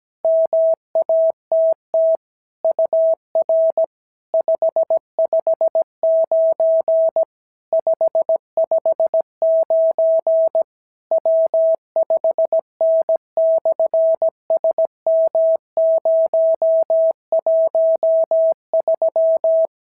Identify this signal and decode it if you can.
Morse code, heard as a keyed tone, about 17 wpm, 650 Hz. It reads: MATT UR 559 559 W5N/SM013
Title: SOTA QSO Practice 17wpm